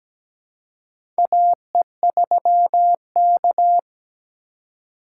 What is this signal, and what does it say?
Morse code, heard as a keyed tone, about 17 wpm, 700 Hz. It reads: AE3K
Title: USA Call Signs - 4 Characters - 3x Character Spacing - 17wpm